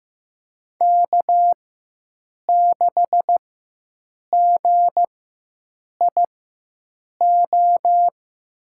Morse code, heard as a keyed tone, about 15 wpm, spaced as for 7 wpm, 700 Hz. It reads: K6GIO
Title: USA Call Signs - 5 Characters - 4x Character Spacing - 15wpm